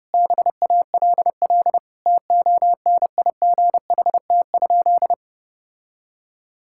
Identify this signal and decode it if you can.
Morse code, heard as a keyed tone, about 30 wpm, 700 Hz. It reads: BALL TONIGHT?
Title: Sentences from Top 1000 Words - Review Part 6 - 30wpm